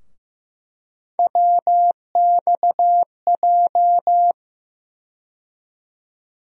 Morse code, heard as a keyed tone, about 15 wpm, 700 Hz. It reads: WXJ
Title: USA Call Sign Postfixes - 3 Letters 15wpm